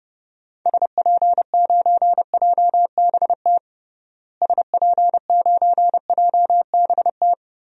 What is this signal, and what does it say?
Morse code, heard as a keyed tone, about 30 wpm, 700 Hz. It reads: SP9JBT SP9JBT
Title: DXCC-100 Call Signs - 6 Characters - Repeated 3x - 30wpm